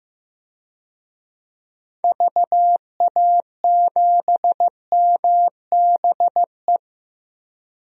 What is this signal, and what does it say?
Morse code, heard as a keyed tone, about 15 wpm, 700 Hz. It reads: VA7MBE
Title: DXCC-100 Call Signs - 6 Characters - Repeated 2x - 15wpm